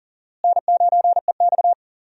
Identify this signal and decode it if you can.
Morse code, heard as a keyed tone, about 40 wpm, 700 Hz. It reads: N9EX
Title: USA Call Signs - 4 Characters - Repeated 3x - 40wpm